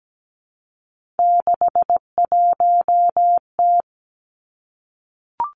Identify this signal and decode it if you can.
Morse code, heard as a keyed tone, about 17 wpm, 700 Hz. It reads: 61T
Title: Three Challenging Letter-Number Combinations 17wpm